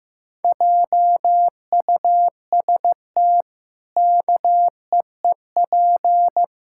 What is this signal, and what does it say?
Morse code, heard as a keyed tone, about 15 wpm, 700 Hz. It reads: JUST KEEP
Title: Pangram Challenge v2 - 15wpm